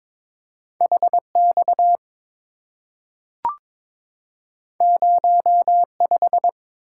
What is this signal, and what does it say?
Morse code, heard as a keyed tone, about 22 wpm, 690 Hz. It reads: HX 05
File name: practice / 22wpm / Two Letter-Number Combinations - 22wpm